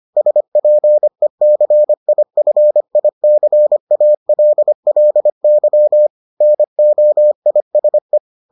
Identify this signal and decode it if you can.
Morse code, heard as a keyed tone, about 25 wpm, 600 Hz. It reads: SPECIFICALLY NOISE